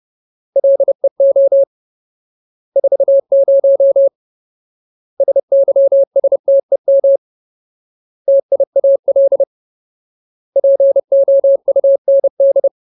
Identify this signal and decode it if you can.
Morse code, heard as a keyed tone, about 30 wpm, 550 Hz. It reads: LEO 40 SYSTEM TIAL POUND
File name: Ultimate Word List / 30WPM Ultimate